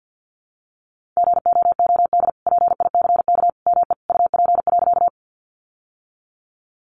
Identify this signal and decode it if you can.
Morse code, heard as a keyed tone, about 50 wpm, 700 Hz. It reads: GOOD PICK ME UP.
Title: Sentences from Top 1000 Words 50wpm